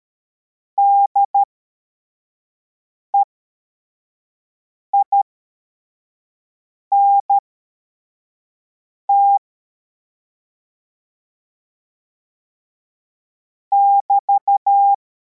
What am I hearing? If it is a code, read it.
Morse code, heard as a keyed tone, about 13 wpm, spaced as for 4 wpm, 800 Hz. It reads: DEINT <BT>
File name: formation / cw / Text33